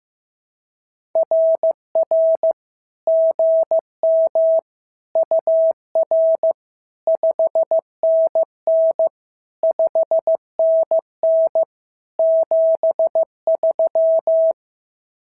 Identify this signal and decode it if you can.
Morse code, heard as a keyed tone, about 15 wpm, 650 Hz. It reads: RR GM UR 5NN 5NN 73
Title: SOTA QSO Practice 15wpm